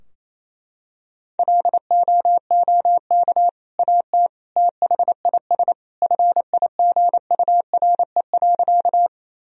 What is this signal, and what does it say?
Morse code, heard as a keyed tone, about 27 wpm, 700 Hz. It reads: LOOK AT THIS FIGURE.
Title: Sentences from Top 1000 Words - Review Part 2 - 28wpm